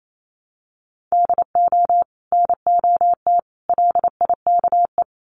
Morse code, heard as a keyed tone, about 28 wpm, 700 Hz. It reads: DO NOT LIKE
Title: Sets of 3 Words 28wpm